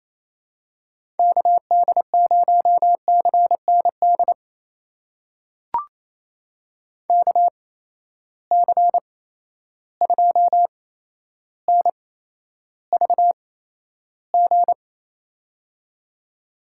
Morse code, heard as a keyed tone, about 28 wpm, spaced as for 12 wpm, 700 Hz. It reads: KD0CND KC2NVG
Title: USA Call Signs - 6 Characters - 8x Character Spacing - 28wpm